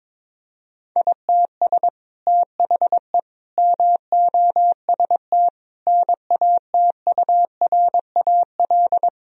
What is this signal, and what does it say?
Morse code, heard as a keyed tone, about 22 wpm, 700 Hz. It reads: ITS THE MOST NATURAL
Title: Sentences from Top 1000 Words - Review Part 4 - 22wpm